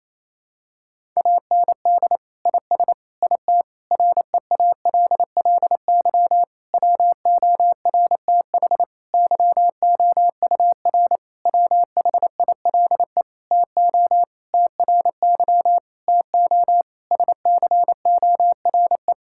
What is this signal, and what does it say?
Morse code, heard as a keyed tone, about 28 wpm, 700 Hz. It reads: AND IS IT REALLY WORTH YOUR WHILE TO TRY TO SCORE
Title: Sentences from Top 1600 Words 28wpm